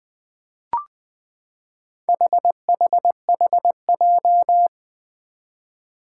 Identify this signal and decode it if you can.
Morse code, heard as a keyed tone, about 20 wpm, 700 Hz. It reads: HHHJ
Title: Four Challenging Letter-Number Combinations 20wpm